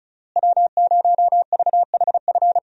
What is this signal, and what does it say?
Morse code, heard as a keyed tone, about 36 wpm, 700 Hz. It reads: W0VHF
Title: USA Call Signs - 5 Characters - Repeated 3x - 35wpm